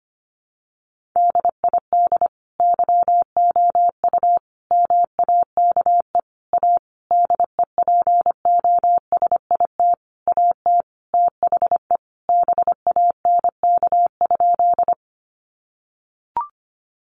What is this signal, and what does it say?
Morse code, heard as a keyed tone, about 25 wpm, 700 Hz. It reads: DID YOU MAKE A DEPOSIT AT THE BANK?
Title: Sentences from Top 1300 Words 25wpm